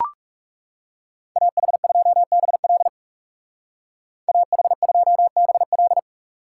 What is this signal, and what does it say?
Morse code, heard as a keyed tone, about 40 wpm, 700 Hz. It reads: AH2BL AH2BL